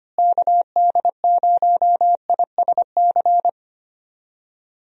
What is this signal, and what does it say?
Morse code, heard as a keyed tone, about 25 wpm, 700 Hz. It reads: KD0ISC
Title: USA Call Signs - 6 Characters - Repeated 3x - 25wpm